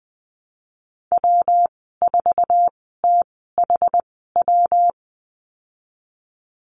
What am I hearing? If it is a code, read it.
Morse code, heard as a keyed tone, about 20 wpm, spaced as for 14 wpm, 700 Hz. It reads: W4THW
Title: 128 Lesson 21g - Number 6 - Call Signs 20wpm